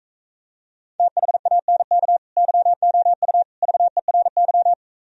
Morse code, heard as a keyed tone, about 44 wpm, 700 Hz. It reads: THANK YOU VERY